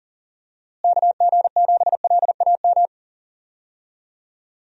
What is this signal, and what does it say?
Morse code, heard as a keyed tone, about 40 wpm, 700 Hz. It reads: KG7LAM
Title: USA Call Signs - 6 Characters - Repeated 3x - 40wpm